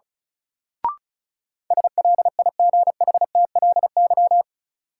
Morse code, heard as a keyed tone, about 36 wpm, 700 Hz. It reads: SLIGHTLY
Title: Sentences from Top 1300 Words - New Word Drill Alt - 35wpm